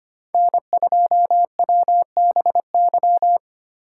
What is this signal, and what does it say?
Morse code, heard as a keyed tone, about 25 wpm, 700 Hz. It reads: N2WBY
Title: USA Call Signs - 4x Character Spacing - 25wpm